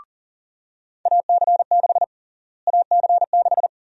Morse code, heard as a keyed tone, about 37 wpm, 700 Hz. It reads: AC6 AC6
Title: USA Call Sign Prefixes - Repeated 2x - 40wpm